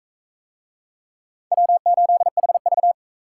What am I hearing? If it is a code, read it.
Morse code, heard as a keyed tone, about 40 wpm, 700 Hz. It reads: W8HV